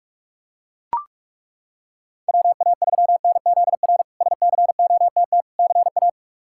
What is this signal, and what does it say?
Morse code, heard as a keyed tone, about 42 wpm, 700 Hz. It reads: WA3NZR SCOTT CA
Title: K1USN SST QSO Practice - 45wpm